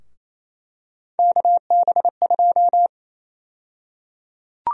Morse code, heard as a keyed tone, about 28 wpm, 700 Hz. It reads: KB2
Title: USA Call Sign Prefixes - 28wpm